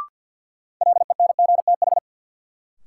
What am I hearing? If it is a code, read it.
Morse code, heard as a keyed tone, about 52 wpm, 700 Hz. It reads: LENGTH
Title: Sentences from Top 1000 Words - New Word Drill - 50wpm